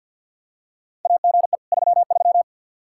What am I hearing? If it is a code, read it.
Morse code, heard as a keyed tone, about 53 wpm, 700 Hz. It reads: AGE 33
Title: Practice QSO - Exchange 3 - More Info - 50wpm